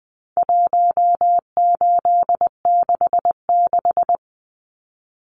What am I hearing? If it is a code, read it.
Morse code, heard as a keyed tone, about 20 wpm, 700 Hz. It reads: 1866
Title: Numbers - 4-Digit 20wpm